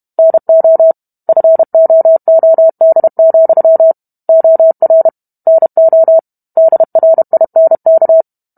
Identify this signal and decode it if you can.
Morse code, heard as a keyed tone, about 32 wpm, 650 Hz. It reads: NO FOOD, OR NO DRINK